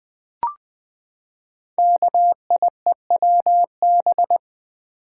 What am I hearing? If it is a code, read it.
Morse code, heard as a keyed tone, about 20 wpm, 700 Hz. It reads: KIEWB
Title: Code Groups - 5 Letters - 20wpm